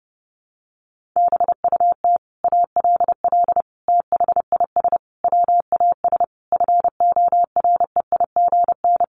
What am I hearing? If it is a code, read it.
Morse code, heard as a keyed tone, about 30 wpm, 700 Hz. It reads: BUT ALL THIS WAS FOREIGN
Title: Sentences from Top 1400 Words 30wpm